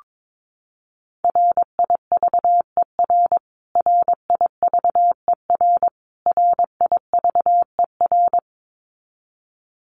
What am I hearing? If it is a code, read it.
Morse code, heard as a keyed tone, about 22 wpm, 700 Hz. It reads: RIVER RIVER RIVER
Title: Top 500 Words Repeated 22wpm